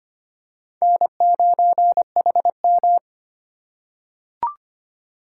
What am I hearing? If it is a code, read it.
Morse code, heard as a keyed tone, about 25 wpm, 700 Hz. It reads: N9HM
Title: USA Call Signs - 4 Characters - 4x Character Spacing - 25wpm